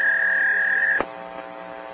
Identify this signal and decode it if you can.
Fax machine handshake.
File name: Hotel Calls / hotel fax machine violation